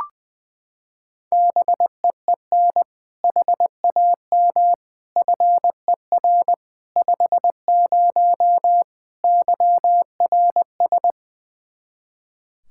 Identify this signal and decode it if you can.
Morse code, heard as a keyed tone, about 21 wpm, 700 Hz. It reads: BEEN HAM FER 50 YRS
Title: K1USN SST QSO Practice 20wpm